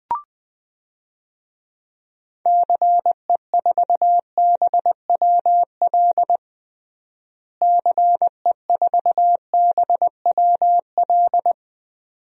Morse code, heard as a keyed tone, about 21 wpm, 700 Hz. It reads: CE4BWL CE4BWL
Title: DXCC-100 Call Signs - 6 Characters - Repeated 2x - 20wpm